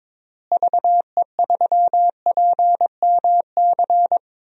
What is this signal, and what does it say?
Morse code, heard as a keyed tone, about 22 wpm, 700 Hz. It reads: VE3PMC
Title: DXCC-100 Call Signs - 6 Characters - Repeated 3x - 22wpm